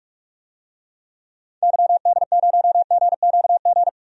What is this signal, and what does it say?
Morse code, heard as a keyed tone, about 45 wpm, 700 Hz. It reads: YD0GQG
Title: DXCC-100 Call Signs - 6 Characters - 45wpm